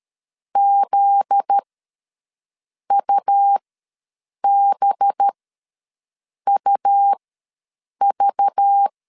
Morse code, heard as a keyed tone, about 13 wpm, spaced as for 6 wpm, 800 Hz. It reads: ZUBUV
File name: LEZIONI / 7 GF9Z / 7 GF9Z ws 12wpm 3321